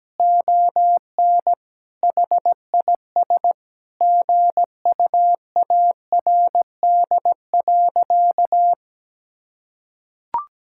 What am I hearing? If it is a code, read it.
Morse code, heard as a keyed tone, about 17 wpm, 700 Hz. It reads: ON HIS GUARD.
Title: Sentences from Top 1000 Words - Review Part 5 - 17wpm